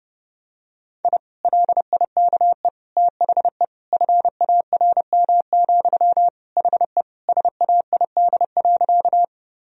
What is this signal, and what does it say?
Morse code, heard as a keyed tone, about 31 wpm, 700 Hz. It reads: I LIKE THE FARM, HE SAID.
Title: Sentences from Top 1000 Words - Review Part 2 - 30wpm